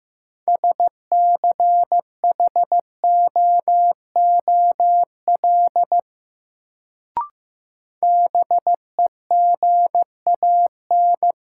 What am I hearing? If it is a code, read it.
Morse code, heard as a keyed tone, about 15 wpm, 700 Hz. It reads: SCHOOL BEGAN
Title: Sentences from Top 300 Words - New Word Drill - 15wpm